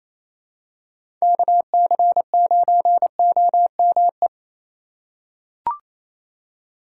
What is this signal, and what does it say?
Morse code, heard as a keyed tone, about 28 wpm, 700 Hz. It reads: KC9OME
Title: USA Call Signs - 4x Character Spacing - 28wpm